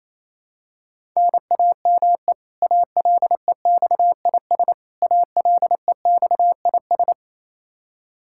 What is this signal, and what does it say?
Morse code, heard as a keyed tone, about 28 wpm, 700 Hz. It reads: NAME ALEXIS ALEXIS
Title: Practice QSO - Exchange 1 - Beginning - 28wpm